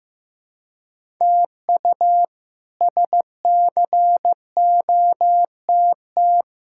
Morse code, heard as a keyed tone, about 15 wpm, 700 Hz. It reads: TU SCOTT
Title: K1USN SST QSO Practice - 15wpm